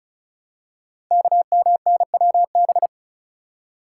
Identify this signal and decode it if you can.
Morse code, heard as a keyed tone, about 36 wpm, 700 Hz. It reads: KMNWB
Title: Code Groups - 5 Letters - 35wpm